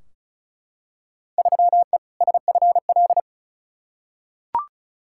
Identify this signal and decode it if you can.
Morse code, heard as a keyed tone, about 34 wpm, 700 Hz. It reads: 3E SFL
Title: Field Day Exchanges Encore - 35wpm